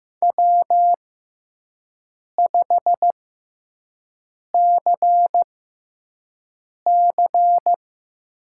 Morse code, heard as a keyed tone, about 15 wpm, spaced as for 5 wpm, 700 Hz. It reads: W5CC
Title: USA Call Signs - 6x Character Spacing - 15wpm